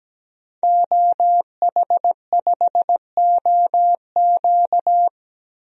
Morse code, heard as a keyed tone, about 17 wpm, 700 Hz. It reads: OH5OQ
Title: DXCC-100 Call Signs - 5 Characters - Repeated 3x - 17wpm